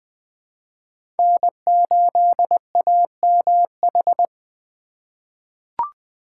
Morse code, heard as a keyed tone, about 20 wpm, 700 Hz. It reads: N8AMH
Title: USA Call Signs - 3x Character Spacing - 20wpm